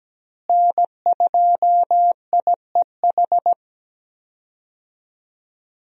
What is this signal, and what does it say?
Morse code, heard as a keyed tone, about 17 wpm, 700 Hz. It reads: N2IEH
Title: USA Call Signs - 5 Characters - Repeated 3x - 17wpm